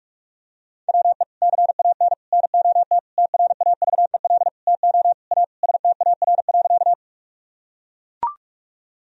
Morse code, heard as a keyed tone, about 45 wpm, 700 Hz. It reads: WE CAN NOT TRAVEL TO A STAR.
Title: Sentences from Top 1000 Words - Review Part 2 - 45wpm